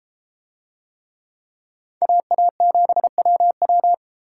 Morse code, heard as a keyed tone, about 32 wpm, 700 Hz. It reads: AA7WW